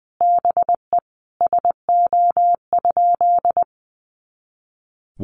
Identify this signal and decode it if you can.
Morse code, heard as a keyed tone, about 20 wpm, 700 Hz. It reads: BE SO?
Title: Sentences from Top 300 Words 20wpm